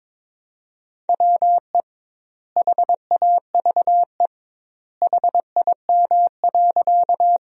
Morse code, heard as a keyed tone, about 21 wpm, 700 Hz. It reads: WE HAVE HIM.
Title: 2x Word-Spacing - Sentences from Top 100 Words - 22wpm